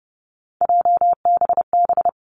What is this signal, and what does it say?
Morse code, heard as a keyed tone, about 30 wpm, 700 Hz. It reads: JBB
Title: Three Challenging Letter-Number Combinations 30wpm